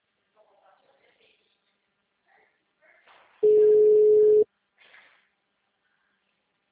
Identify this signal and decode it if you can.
Ringback tone (Europe, 425 Hz), with speech.
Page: Marilia - SP